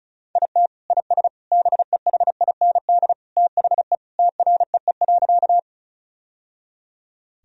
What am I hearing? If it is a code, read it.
Morse code, heard as a keyed tone, about 36 wpm, 700 Hz. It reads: IT IS BEHIND THE TREE.
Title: Sentences from Top 1000 Words - Review Part 1 - 35wpm